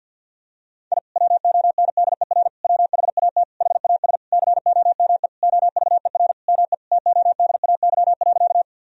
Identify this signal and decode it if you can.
Morse code, heard as a keyed tone, about 50 wpm, 700 Hz. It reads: I WONDER WHAT HAS COME OVER ME TODAY.
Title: Sentences from Top 1000 Words - Review Part 4 - 50wpm